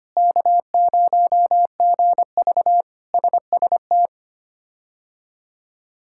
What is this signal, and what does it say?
Morse code, heard as a keyed tone, about 25 wpm, 700 Hz. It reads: K0GV SST
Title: K1USN SST QSO Practice - 25wpm